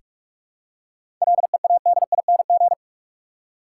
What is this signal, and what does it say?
Morse code, heard as a keyed tone, about 42 wpm, 700 Hz. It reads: LEADING
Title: Seven Letter Words - 45wpm